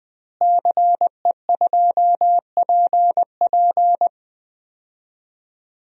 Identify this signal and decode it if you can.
Morse code, heard as a keyed tone, about 20 wpm, 700 Hz. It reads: CE2PP
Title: DXCC-100 Call Signs - 5 Characters - Repeated 3x - 20wpm